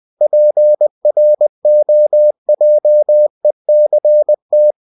Morse code, heard as a keyed tone, about 20 wpm, 600 Hz. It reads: PROJECT